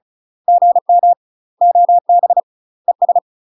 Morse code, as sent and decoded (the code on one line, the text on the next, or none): --. -- / --- -... / . ...
GM OB ES